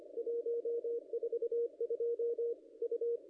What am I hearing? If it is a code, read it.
Morse code, heard as a keyed tone, about 25 wpm, 440 Hz. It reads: 142 U
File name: contests / ss-cw-k5tr-op-n5rz -0200